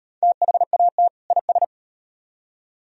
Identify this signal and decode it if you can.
Morse code, heard as a keyed tone, about 34 wpm, 700 Hz. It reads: THAT IS